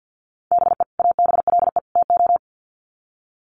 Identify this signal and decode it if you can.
Morse code, heard as a keyed tone, about 48 wpm, 700 Hz. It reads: BE ABLE TO
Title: Sets of 3 Words 50wpm